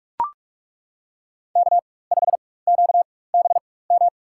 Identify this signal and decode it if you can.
Morse code, heard as a keyed tone, about 41 wpm, spaced as for 20 wpm, 700 Hz. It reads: K5QBM
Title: USA Call Signs - 5 Characters - 4x Character Spacing - 45wpm